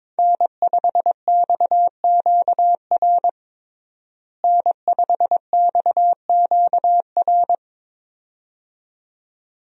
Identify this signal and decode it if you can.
Morse code, heard as a keyed tone, about 22 wpm, 700 Hz. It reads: N5XQR N5XQR
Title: USA Call Signs - 5 Characters - Repeated 2x - 22wpm